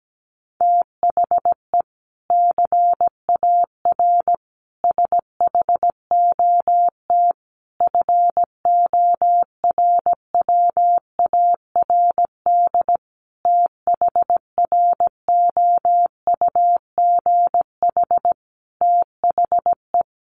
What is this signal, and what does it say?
Morse code, heard as a keyed tone, about 17 wpm, 700 Hz. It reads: THE CAR SHOT FORWARD THROUGH THE
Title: Sentences from Top 1300 Words 17wpm